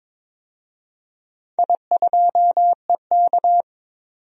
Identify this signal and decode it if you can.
Morse code, heard as a keyed tone, about 22 wpm, 700 Hz. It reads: I2EK
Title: DXCC-100 Call Signs - 4 Characters - 22wpm